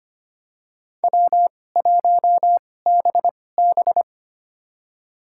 Morse code, heard as a keyed tone, about 25 wpm, spaced as for 18 wpm, 700 Hz. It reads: W1BB
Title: USA Call Signs - 4 Characters - 2x Character Spacing - 25wpm